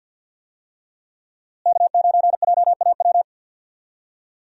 Morse code, heard as a keyed tone, about 52 wpm, 700 Hz. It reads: K9JAW
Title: USA Call Signs - 5 Characters - Repeated 3x - 50wpm